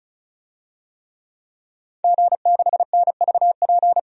Morse code, heard as a keyed tone, about 34 wpm, 700 Hz. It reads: G6NVP
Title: DXCC-100 Call Signs - 5 Characters - 35wpm